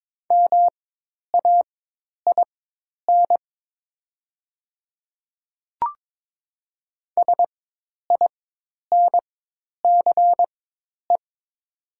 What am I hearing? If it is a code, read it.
Morse code, heard as a keyed tone, about 22 wpm, spaced as for 10 wpm, 700 Hz. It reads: MAIN SINCE
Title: 4x Character Spacing - Top 300 Words - 22wpm